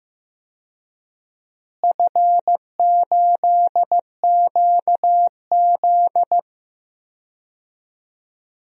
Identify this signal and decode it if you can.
Morse code, heard as a keyed tone, about 15 wpm, 700 Hz. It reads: F8QZ
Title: DXCC-100 Call Signs - 4 Characters - Repeated 2x - 15wpm